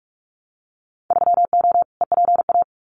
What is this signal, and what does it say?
Morse code, heard as a keyed tone, about 46 wpm, 700 Hz. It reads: 3O EPA
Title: Winter Field Day Exchanges 45wpm